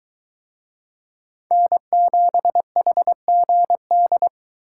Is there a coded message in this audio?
Yes: Morse code from 1.5 to 4.3 s, heard as a keyed tone, about 22 wpm, 700 Hz. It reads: N7HGD